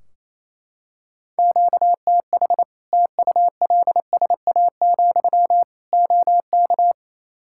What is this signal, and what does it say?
Morse code, heard as a keyed tone, about 28 wpm, 700 Hz. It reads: QTH TULSA, OK
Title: Practice QSO - Exchange 1 - Beginning - 28wpm